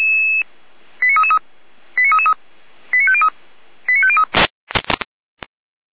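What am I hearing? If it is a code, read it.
Five-tone selective call (ZVEI), heard as quick runs of five tones, tones of about 70 ms. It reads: 89333 89333 89503 89503
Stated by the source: ZVEI five-tone call received by SDR.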